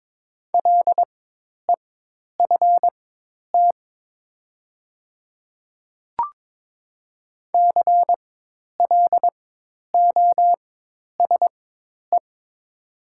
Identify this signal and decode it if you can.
Morse code, heard as a keyed tone, about 22 wpm, spaced as for 10 wpm, 700 Hz. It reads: LEFT CLOSE
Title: 4x Character Spacing - Top 300 Words - 22wpm